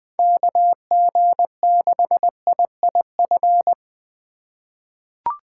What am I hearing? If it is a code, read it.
Morse code, heard as a keyed tone, about 20 wpm, 700 Hz. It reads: KG6IIF
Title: USA Call Signs - 2x Character Spacing - 20wpm